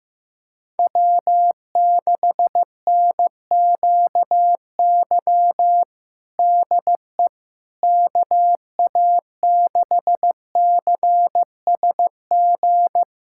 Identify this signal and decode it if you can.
Morse code, heard as a keyed tone, about 15 wpm, 700 Hz. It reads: W6NQY DE KA6CSG
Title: USA Callsign DE Callsign 15wpm